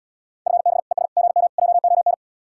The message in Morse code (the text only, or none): FB ES GM JON